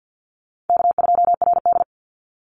Morse code, heard as a keyed tone, about 50 wpm, 700 Hz. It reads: K2RD
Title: Easy Call Signs Repeated 2x - 50wpm